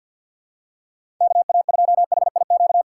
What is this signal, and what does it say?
Morse code, heard as a keyed tone, about 49 wpm, 700 Hz. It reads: KA2HIQ
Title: USA Call Signs - 6 Characters - 3x Character Spacing - 50wpm